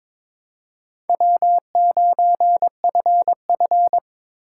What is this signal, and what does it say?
Morse code, heard as a keyed tone, about 26 wpm, 700 Hz. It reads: W9FF
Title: Speed Racing - USA Call Signs 22wpm